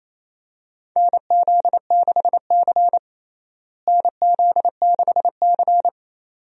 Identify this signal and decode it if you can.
Morse code, heard as a keyed tone, about 28 wpm, 700 Hz. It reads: NZ6C NZ6C
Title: USA Call Signs - 4 Characters - Repeated 2x - 28wpm